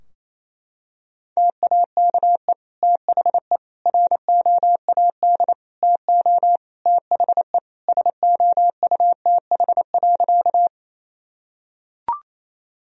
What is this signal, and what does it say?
Morse code, heard as a keyed tone, about 28 wpm, 700 Hz. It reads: TAKE THE ROAD TO THE SOUTH.
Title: Sentences from Top 1000 Words - Review Part 2 - 28wpm